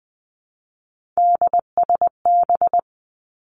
Morse code, heard as a keyed tone, about 20 wpm, 700 Hz. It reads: DSB
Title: Three Challenging Letter-Number Combinations 20wpm